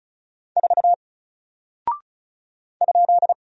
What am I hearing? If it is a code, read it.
Morse code, heard as a keyed tone, about 35 wpm, 700 Hz. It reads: 4 ?
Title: Single Character (Letter:Number:Punctuation:Prosign) 35wpm